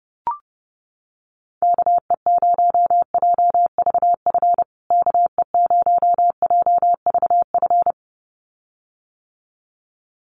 Morse code, heard as a keyed tone, about 30 wpm, 700 Hz. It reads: KE0JVF KE0JVF
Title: USA Call Signs Repeated 2x 30wpm